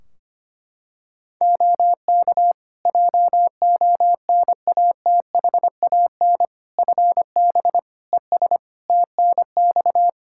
Morse code, heard as a keyed tone, about 25 wpm, 700 Hz. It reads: OK JONATHAN FB ES TNX
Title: Practice QSO - Exchange 2 - Info - 25wpm